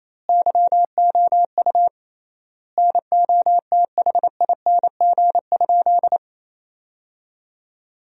Morse code, heard as a keyed tone, about 27 wpm, 700 Hz. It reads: YOU NOTHING?
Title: 3x Word-Spacing - Sentences from Top 500 Words - 28wpm